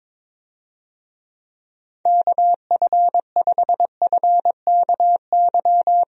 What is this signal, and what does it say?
Morse code, heard as a keyed tone, about 21 wpm, 700 Hz. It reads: KF5FKY
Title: USA Call Signs - 6 Characters - Repeated 2x - 22wpm